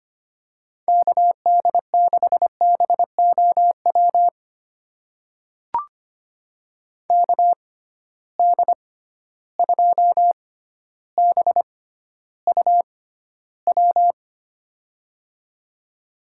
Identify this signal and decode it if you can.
Morse code, heard as a keyed tone, about 25 wpm, spaced as for 13 wpm, 700 Hz. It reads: KD6BOW KD2BUW
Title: USA Call Signs - 6 Characters - 6x Character Spacing - 25wpm